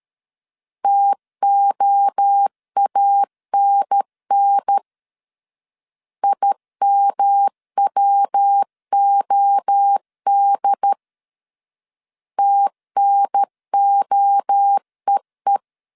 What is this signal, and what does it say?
Morse code, heard as a keyed tone, about 13 wpm, 800 Hz. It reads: TOANN IMWOD TNOEE
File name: CW / DET / 1 3 1500C S C 12wpm 2750